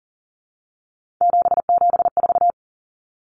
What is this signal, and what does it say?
Morse code, heard as a keyed tone, about 40 wpm, 700 Hz. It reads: 774
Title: Common Cut Numbers - 3-Digit 40wpm